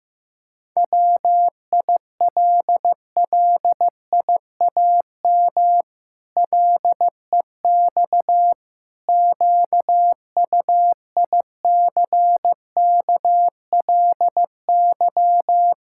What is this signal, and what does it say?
Morse code, heard as a keyed tone, about 15 wpm, 700 Hz. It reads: WILLIAM LEX QUICKLY
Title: Pangram Challenge v2 - 15wpm